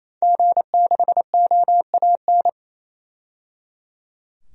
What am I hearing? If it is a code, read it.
Morse code, heard as a keyed tone, about 28 wpm, 700 Hz. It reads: G6OAN
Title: DXCC-100 Call Signs - 5 Characters - 28wpm